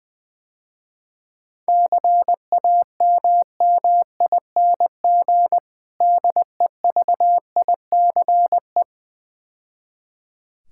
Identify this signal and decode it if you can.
Morse code, heard as a keyed tone, about 20 wpm, 700 Hz. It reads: CAMMING DEVICE
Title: Difficult Words - Speed-Racing with Hints - 20wpm